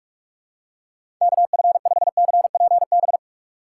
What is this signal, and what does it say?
Morse code, heard as a keyed tone, about 46 wpm, 700 Hz. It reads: KF5CPB
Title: USA Call Signs - 6 Characters - 2x Character Spacing - 45wpm